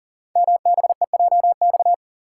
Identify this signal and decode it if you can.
Morse code, heard as a keyed tone, about 40 wpm, 700 Hz. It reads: MBEJX
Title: Code Groups - 5 Letters - 40wpm